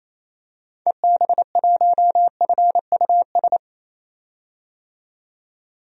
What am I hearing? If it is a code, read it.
Morse code, heard as a keyed tone, about 27 wpm, 700 Hz. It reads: EB1FUS
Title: DXCC-100 Call Signs - 6 Characters - 28wpm